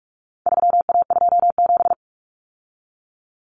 Morse code, heard as a keyed tone, about 44 wpm, 700 Hz. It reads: 3A27
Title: Common Cut Numbers - 4-Digit 45wpm